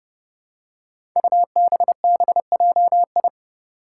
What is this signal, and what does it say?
Morse code, heard as a keyed tone, about 30 wpm, 700 Hz. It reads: UBBJI
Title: Five Challenging Letter-Number Combinations 30wpm